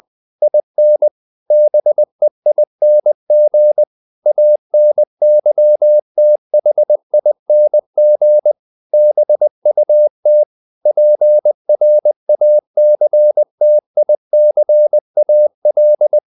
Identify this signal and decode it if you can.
Morse code, heard as a keyed tone, about 20 wpm, 600 Hz. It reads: IN BEING ANYTHING BUT PRACTICAL
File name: the old man and the sea / the old man and the sea part 1 20wpm